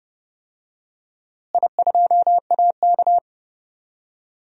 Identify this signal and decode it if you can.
Morse code, heard as a keyed tone, about 30 wpm, 700 Hz. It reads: I2AK
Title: DXCC-100 Call Signs - 4 Characters - Repeated 3x - 30wpm